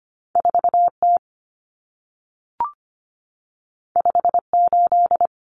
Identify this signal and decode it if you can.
Morse code, heard as a keyed tone, about 25 wpm, 700 Hz. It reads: 4T 58
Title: Common Cut Numbers - 2-Digit 25wpm